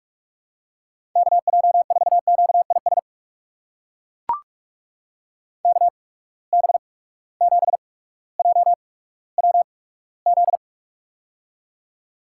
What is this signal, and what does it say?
Morse code, heard as a keyed tone, about 42 wpm, spaced as for 19 wpm, 700 Hz. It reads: KJ4QIS KB7JWZ
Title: USA Call Signs - 6 Characters - 8x Character Spacing - 45wpm